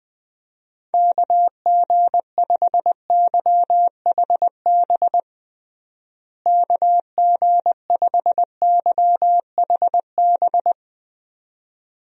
Morse code, heard as a keyed tone, about 21 wpm, 700 Hz. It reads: KG5YHB KG5YHB
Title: USA Call Signs - 6 Characters - Repeated 2x - 20wpm